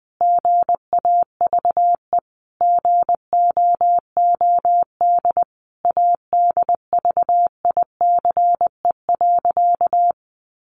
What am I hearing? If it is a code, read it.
Morse code, heard as a keyed tone, about 19 wpm, 700 Hz. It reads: GAVE GOOD ADVICE.
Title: Sentences from Top 1100 Words 20wpm